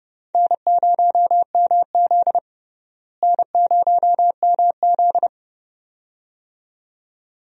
Morse code, heard as a keyed tone, about 29 wpm, 700 Hz. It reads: N0MZ N0MZ
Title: USA Call Signs - 4 Characters - Repeated 2x - 30wpm